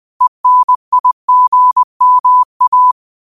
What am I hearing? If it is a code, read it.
Morse code, heard as a keyed tone, about 20 wpm, 1000 Hz. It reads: ENIGMA
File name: qotdfiles / 20wpm